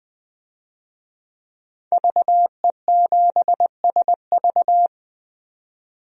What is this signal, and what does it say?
Morse code, heard as a keyed tone, about 21 wpm, 700 Hz. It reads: VE7SV
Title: DXCC-100 Call Signs - 5 Characters - 20wpm